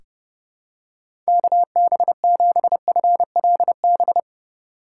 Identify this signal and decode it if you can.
Morse code, heard as a keyed tone, about 30 wpm, 700 Hz. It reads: KB7FLB
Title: USA Call Signs - 2x Character Spacing - 30wpm